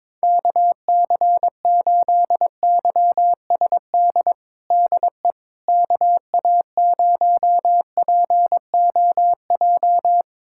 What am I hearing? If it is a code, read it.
Morse code, heard as a keyed tone, about 22 wpm, 700 Hz. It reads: KC8YSD DE KA0POJ
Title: USA Callsign DE Callsign - 6 Characters - 22wpm